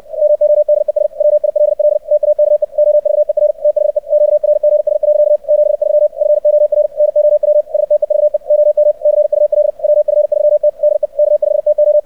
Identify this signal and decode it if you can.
Morse code, heard as a keyed tone, about 64 wpm, 600 Hz. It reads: MODES WILL TAKE OVER THE QRGS. 73 FOR NOW STEVE OM CUL WF3T DE K5TO
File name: archived / CRSnet